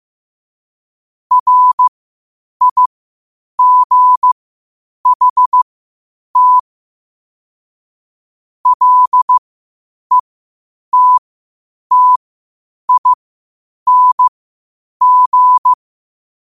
Morse code, heard as a keyed tone, about 15 wpm, spaced as for 8 wpm, 1000 Hz. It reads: RIGHT LETTING
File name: qotdfiles / 8wpm